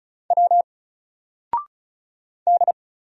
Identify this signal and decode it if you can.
Morse code, heard as a keyed tone, about 35 wpm, 700 Hz. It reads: W D
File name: Single-Letter-Number-35wpm